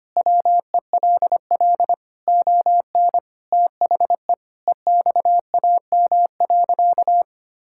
Morse code, heard as a keyed tone, about 25 wpm, 700 Hz. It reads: WELL ON THE EXAM.
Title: Sentences from Top 1000 Words - Review Part 5 - 25wpm